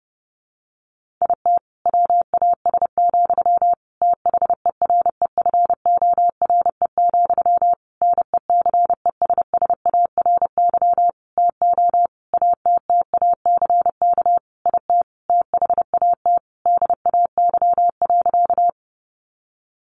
Morse code, heard as a keyed tone, about 30 wpm, 700 Hz. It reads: IT WAS, THEREFORE, NECESSARY TO ATTACK IT THAT DAY.
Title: Sentences from Top 1500 Words 30wpm